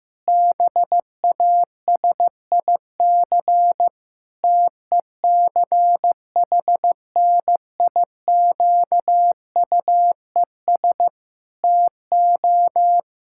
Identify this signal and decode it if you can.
Morse code, heard as a keyed tone, about 15 wpm, 700 Hz. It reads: BASIC TECHNIQUES TO
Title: Pangram Challenge v2 - 15wpm